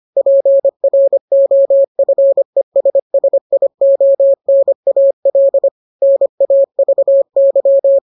Morse code, heard as a keyed tone, about 25 wpm, 550 Hz. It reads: PROFESSIONAL NAVY
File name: jokes / joke 2 25 0000